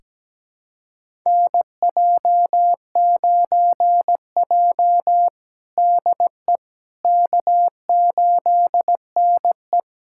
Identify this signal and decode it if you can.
Morse code, heard as a keyed tone, about 17 wpm, 700 Hz. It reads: NJ9J DE K8NE
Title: USA Callsign DE Callsign - 4 Characters - 17wpm